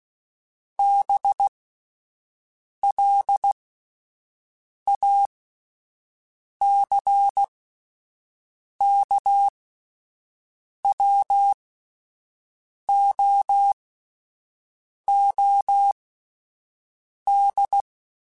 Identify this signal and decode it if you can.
Morse code, heard as a keyed tone, about 16 wpm, spaced as for 5 wpm, 800 Hz. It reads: BLACKWOOD